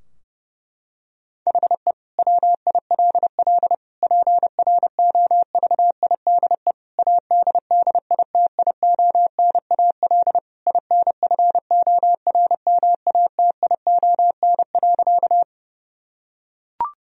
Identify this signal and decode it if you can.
Morse code, heard as a keyed tone, about 31 wpm, 700 Hz. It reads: HE WILL PROVIDE ADDITIONAL INFORMATION.
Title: Sentences from Top 1000 Words - Review Part 4 - 30wpm